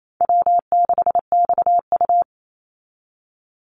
Morse code, heard as a keyed tone, about 28 wpm, 700 Hz. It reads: W6XU
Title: Easy Call Signs Repeated 3x - 28wpm